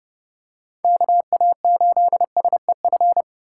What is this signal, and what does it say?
Morse code, heard as a keyed tone, about 30 wpm, 700 Hz. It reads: KA8SEF
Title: USA Call Signs - 6 Characters - 2x Character Spacing - 30wpm